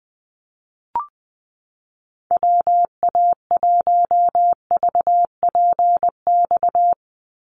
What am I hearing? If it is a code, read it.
Morse code, heard as a keyed tone, about 20 wpm, 700 Hz. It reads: WA1VPX
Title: USA Call Signs 20wpm